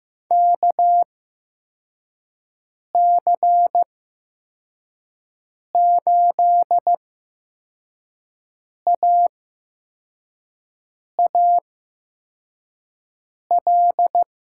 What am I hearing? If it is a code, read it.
Morse code, heard as a keyed tone, about 15 wpm, spaced as for 4 wpm, 700 Hz. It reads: KC8AAL
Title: USA Call Signs - 6 Characters - 8x Character Spacing - 15wpm